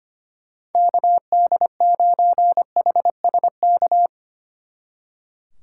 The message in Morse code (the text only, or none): KD9HSK